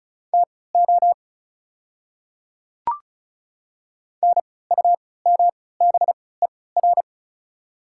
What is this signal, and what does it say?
Morse code, heard as a keyed tone, about 35 wpm, spaced as for 20 wpm, 690 Hz. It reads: TO NUMBER
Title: 3x Character Spacing - Top 100 Words - 35wpm